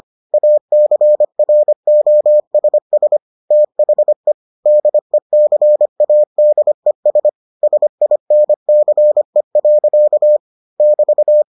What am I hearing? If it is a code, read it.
Morse code, heard as a keyed tone, about 25 wpm, 600 Hz. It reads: ACROSS THE DECADES SINCE. <BT>